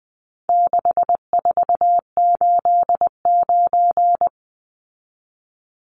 Morse code, heard as a keyed tone, about 20 wpm, 700 Hz. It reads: 6489
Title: Numbers - 4-Digit 20wpm